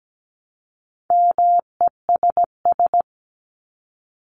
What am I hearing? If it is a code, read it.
Morse code, heard as a keyed tone, about 17 wpm, 700 Hz. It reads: MESS
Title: Top 1751-2000 Words 17wpm